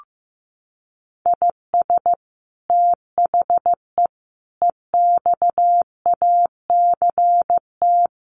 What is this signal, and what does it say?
Morse code, heard as a keyed tone, about 15 wpm, 700 Hz. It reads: IS THE EXACT
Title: Sentences from Top 1300 Words 15wpm